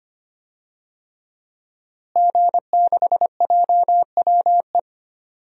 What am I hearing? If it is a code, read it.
Morse code, heard as a keyed tone, about 25 wpm, 700 Hz. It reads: G6JWE
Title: DXCC-100 Call Signs - 5 Characters - Repeated 3x - 25wpm